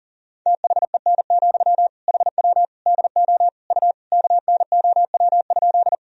Morse code, heard as a keyed tone, about 40 wpm, 700 Hz. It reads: THEN, HW DO U KNOW?
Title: Sentences from Top 100 Words and Abbreviations 40wpm